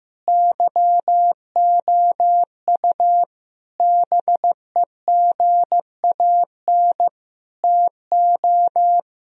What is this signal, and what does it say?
Morse code, heard as a keyed tone, about 15 wpm, 700 Hz. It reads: YOU BEGAN TO
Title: Sentences from Top 1000 Words - Review Part 2 - 15wpm